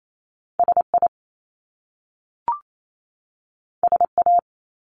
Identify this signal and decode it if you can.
Morse code, heard as a keyed tone, about 28 wpm, 700 Hz. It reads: SI SA
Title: Top Two Letter Combinations in Words 28wpm